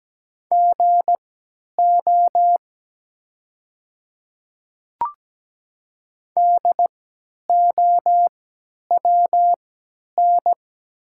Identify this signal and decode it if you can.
Morse code, heard as a keyed tone, about 17 wpm, spaced as for 10 wpm, 700 Hz. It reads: GO DOWN
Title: 3x Character Spacing - Top 100 Words - 17wpm